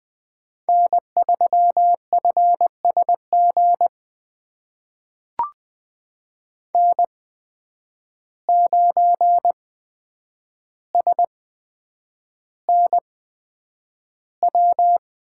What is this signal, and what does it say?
Morse code, heard as a keyed tone, about 20 wpm, spaced as for 9 wpm, 700 Hz. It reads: N3FSG N9SNW
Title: USA Call Signs - 5 Characters - 8x Character Spacing - 20wpm